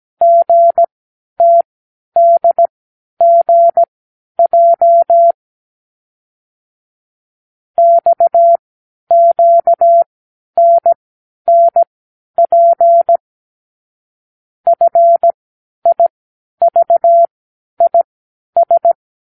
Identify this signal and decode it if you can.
Morse code, heard as a keyed tone, about 17 wpm, spaced as for 10 wpm, 680 Hz. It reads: GTDGJ XQNNP FIVIS
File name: k7qo / Lesson 36 (A-X)